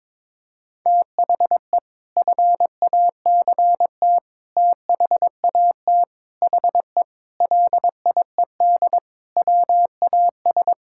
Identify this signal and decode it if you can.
Morse code, heard as a keyed tone, about 22 wpm, 700 Hz. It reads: THE FACT THAT HE LIED WAS
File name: practice / 22wpm / Sentences with 4-Word Phrases - 22wpm